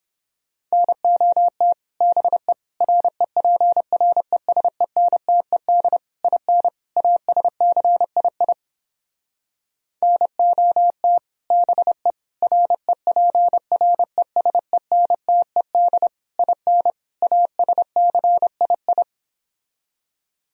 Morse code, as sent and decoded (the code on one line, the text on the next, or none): -. --- - / -... / .-. .--. .-. ... -. - . -.. / .. -. / .- ... -.-. .. .. / -. --- - / -... / .-. .--. .-. ... -. - . -.. / .. -. / .- ... -.-. .. ..
NOT BE REPRESENTED IN ASCII NOT BE REPRESENTED IN ASCII